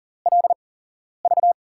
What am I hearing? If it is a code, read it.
Morse code, heard as a keyed tone, about 38 wpm, spaced as for 11 wpm, 700 Hz. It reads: LV
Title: USA Call Signs - 5 Characters - 8x Character Spacing - 40wpm